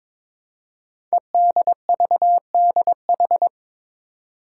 Morse code, heard as a keyed tone, about 23 wpm, 700 Hz. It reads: EDVDH
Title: Code Groups - 5 Letters - 22wpm